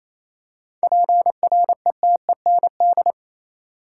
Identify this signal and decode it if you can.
Morse code, heard as a keyed tone, about 28 wpm, 700 Hz. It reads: PRETEND
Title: Seven Letter Words - 28wpm